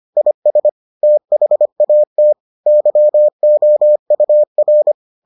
Morse code, heard as a keyed tone, about 25 wpm, 600 Hz. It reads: IS THAT YOUR